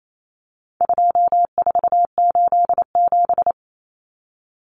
Morse code, heard as a keyed tone, about 28 wpm, 700 Hz. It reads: 2487
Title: Numbers - 4-Digit 28wpm